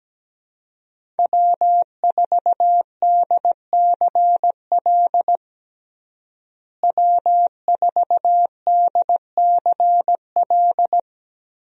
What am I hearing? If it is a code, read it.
Morse code, heard as a keyed tone, about 17 wpm, 700 Hz. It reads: W4DCL W4DCL
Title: USA Call Signs - 5 Characters - Repeated 2x - 17wpm